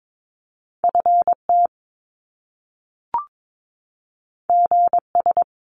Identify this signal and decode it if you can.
Morse code, heard as a keyed tone, about 22 wpm, 700 Hz. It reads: FT GS
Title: Top Two Letter Combinations in Words 22wpm